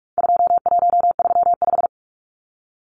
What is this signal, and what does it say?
Morse code, heard as a keyed tone, about 44 wpm, 700 Hz. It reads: 2135
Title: Numbers - 4-Digit 45wpm